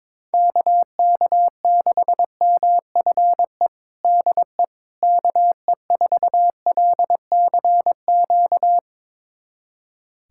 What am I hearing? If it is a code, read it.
Morse code, heard as a keyed tone, about 22 wpm, 700 Hz. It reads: KK6MFE DE KE4LCQ
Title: USA Callsign DE Callsign - 6 Characters - 22wpm